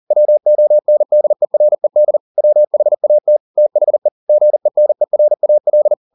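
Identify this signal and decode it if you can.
Morse code, heard as a keyed tone, about 40 wpm, 600 Hz. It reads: WONDERED WHAT THE GENERAL
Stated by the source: Morse code at 40 wpm, practice text, CW med Gnister.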